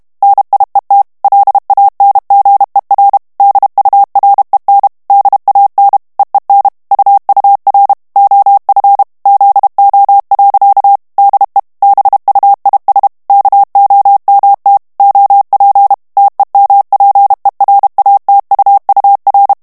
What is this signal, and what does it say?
Morse code, heard as a keyed tone, about 32 wpm, 800 Hz. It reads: NIET LANGER DUREN DAN EEN UUR OF ZO. DE BUIS KOMT OP TEMPERATUUR
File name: HOLLANDAIS texte clair 2 vitesse 30 mots minute